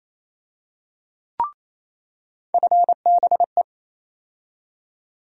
Morse code, heard as a keyed tone, about 28 wpm, 700 Hz. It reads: FBE
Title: USA Call Sign Postfixes - 3 Letters 28wpm